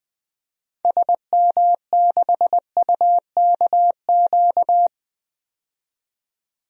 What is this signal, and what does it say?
Morse code, heard as a keyed tone, about 20 wpm, 700 Hz. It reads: SM6UKQ
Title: DXCC-100 Call Signs - 6 Characters - 20wpm